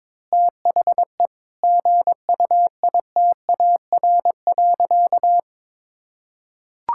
Morse code, heard as a keyed tone, about 22 wpm, 700 Hz. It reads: THE GUITAR.
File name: Difficult Sentences with Hints - 22wpm